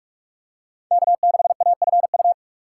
Morse code, heard as a keyed tone, about 44 wpm, 700 Hz. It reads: K6AFU
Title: USA Call Signs - 5 Characters - 3x Character Spacing - 45wpm